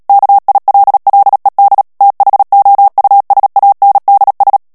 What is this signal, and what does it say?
Morse code, heard as a keyed tone, about 37 wpm, 800 Hz. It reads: KILLED THOUSANDS
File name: morse mp3 / ANGLAIS texte clair 2 vitesse 35 mots minute